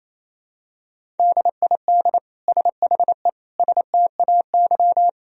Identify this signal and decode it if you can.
Morse code, heard as a keyed tone, about 28 wpm, 700 Hz. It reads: DID SHE STAY
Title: Sentences from Top 500 Words and Abbreviations 28wpm